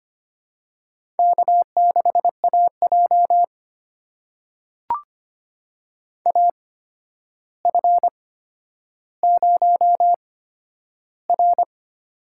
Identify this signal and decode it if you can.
Morse code, heard as a keyed tone, about 25 wpm, spaced as for 11 wpm, 700 Hz. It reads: K6AJ AF0R
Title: USA Call Signs - 4 Characters - 8x Character Spacing - 25wpm